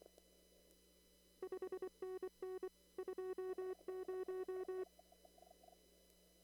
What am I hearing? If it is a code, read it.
Morse code, heard as a keyed tone, about 24 wpm, 370 Hz. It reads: 5NN 20
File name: CW / kt5j wpx cw-op-k5pi